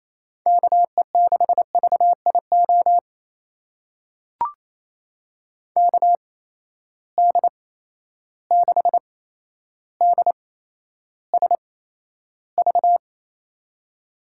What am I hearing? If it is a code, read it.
Morse code, heard as a keyed tone, about 28 wpm, spaced as for 12 wpm, 700 Hz. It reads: KE6VIO KD6DSV
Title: USA Call Signs - 6 Characters - 8x Character Spacing - 28wpm